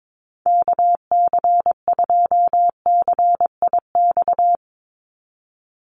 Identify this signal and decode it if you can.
Morse code, heard as a keyed tone, about 22 wpm, 700 Hz. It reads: KC2CIX
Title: USA Call Signs 22wpm